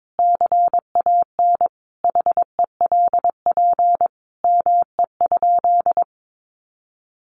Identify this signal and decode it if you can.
Morse code, heard as a keyed tone, about 22 wpm, 700 Hz. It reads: CAN HELP ME?
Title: Sentences from Top 200 Words 22wpm